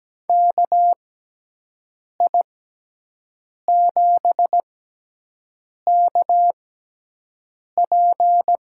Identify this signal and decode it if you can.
Morse code, heard as a keyed tone, about 17 wpm, spaced as for 6 wpm, 700 Hz. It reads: KI7KP
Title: USA Call Signs - 6x Character Spacing - 17wpm